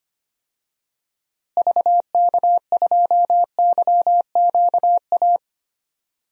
Morse code, heard as a keyed tone, about 25 wpm, 700 Hz. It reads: VK2YQA
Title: DXCC-100 Call Signs - 6 Characters - Repeated 2x - 25wpm